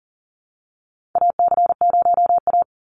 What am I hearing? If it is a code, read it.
Morse code, heard as a keyed tone, about 40 wpm, 700 Hz. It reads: AC0A
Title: NA SOTA Chaser Callsigns 40wpm